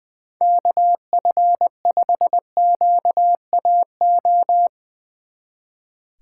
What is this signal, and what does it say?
Morse code, heard as a keyed tone, about 22 wpm, 700 Hz. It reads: KF5QAO
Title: Speed Racing - USA 2x3 Call Signs 20wpm